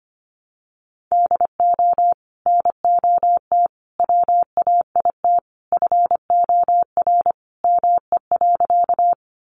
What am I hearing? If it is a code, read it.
Morse code, heard as a keyed tone, about 25 wpm, 700 Hz. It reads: DO NOT WAIT FOR ME.
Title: Sentences from Top 600 Words 25wpm